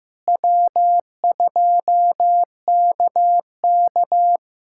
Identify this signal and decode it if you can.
Morse code, heard as a keyed tone, about 15 wpm, 700 Hz. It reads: W2KK
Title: K1USN SST QSO Practice - 15wpm